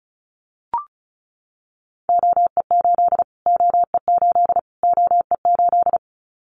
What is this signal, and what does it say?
Morse code, heard as a keyed tone, about 34 wpm, 700 Hz. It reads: OE8 OE8 OE8
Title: Callsign Prefixes for Top 25 DXCC - Repeated 3x - 35wpm